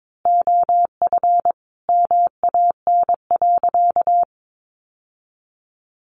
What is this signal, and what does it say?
Morse code, heard as a keyed tone, about 22 wpm, 700 Hz. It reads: OF MAN.
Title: Sentences from Top 1400 Words 22wpm